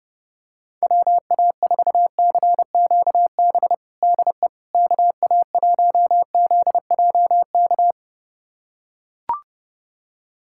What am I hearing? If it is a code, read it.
Morse code, heard as a keyed tone, about 30 wpm, 700 Hz. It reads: WA4CQB DE KA1ZJK
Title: USA Callsign DE Callsign - 6 Characters - 30wpm